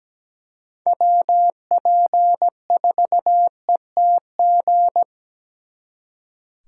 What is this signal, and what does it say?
Morse code, heard as a keyed tone, about 20 wpm, 700 Hz. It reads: WP4ETG
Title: Speed Racing - USA 2x3 Call Signs 17wpm